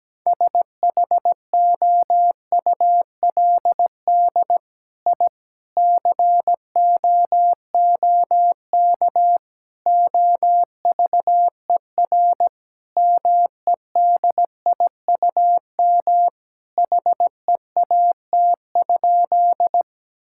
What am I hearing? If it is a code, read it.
Morse code, heard as a keyed tone, about 17 wpm, 700 Hz. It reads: SHOULD I COOK OVER MEDIUM HEAT?
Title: Sentences from Top 1000 Words - Review Part 6 - 17wpm